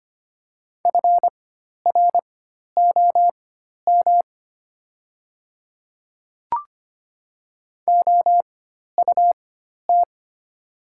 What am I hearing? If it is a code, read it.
Morse code, heard as a keyed tone, about 25 wpm, spaced as for 12 wpm, 700 Hz. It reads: FROM OUT
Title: 4x Character Spacing - Top 200 Words - 25wpm